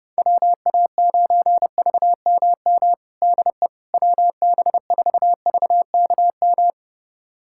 Morse code, heard as a keyed tone, about 30 wpm, 700 Hz. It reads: WA9VMM DE WB4VKM
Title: USA Callsign DE Callsign - 6 Characters - 30wpm